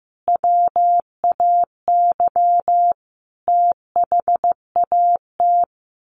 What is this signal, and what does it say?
Morse code, heard as a keyed tone, about 14 wpm, 700 Hz. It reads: WAY THAT
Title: Sets of 4 Words 15wpm